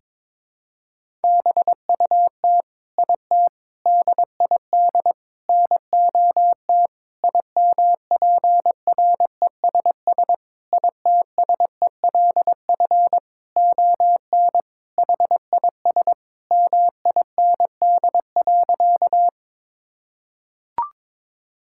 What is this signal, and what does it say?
Morse code, heard as a keyed tone, about 22 wpm, 700 Hz. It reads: BUT IT DID NOT IMPRESS ITSELF ON HIS MIND.
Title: Sentences from Top 1600 Words 22wpm